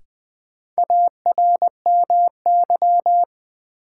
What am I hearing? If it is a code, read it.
Morse code, heard as a keyed tone, about 20 wpm, 700 Hz. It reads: ARMY
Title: Sentences from Top 800 Words - New Word Drill - 20wpm